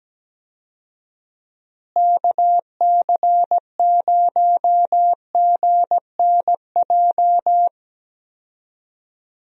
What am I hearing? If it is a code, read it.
Morse code, heard as a keyed tone, about 17 wpm, 700 Hz. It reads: KC0GNJ
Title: USA Call Signs - 6 Characters - Repeated 3x - 17wpm